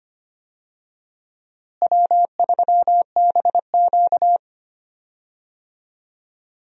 Morse code, heard as a keyed tone, about 25 wpm, 700 Hz. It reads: W3BQ
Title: USA Call Signs - 4 Characters - Repeated 2x - 25wpm